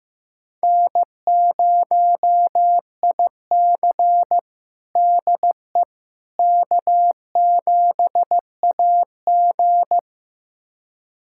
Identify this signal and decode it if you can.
Morse code, heard as a keyed tone, about 15 wpm, 700 Hz. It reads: N0IC DE K7AG
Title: USA Callsign DE Callsign - 4 Characters - 15wpm